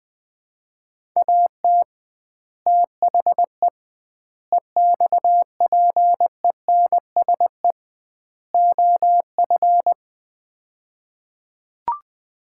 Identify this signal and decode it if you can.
Morse code, heard as a keyed tone, about 20 wpm, 700 Hz. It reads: AT THE EXPENSE OF
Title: 2x Word-Spacing - Sets of 4 Words - 20wpm